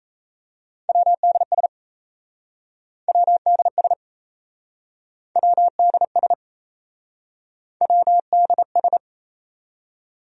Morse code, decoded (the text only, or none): WDS WDS WDS WDS